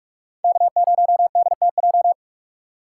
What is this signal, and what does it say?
Morse code, heard as a keyed tone, about 44 wpm, 700 Hz. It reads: K0DTJ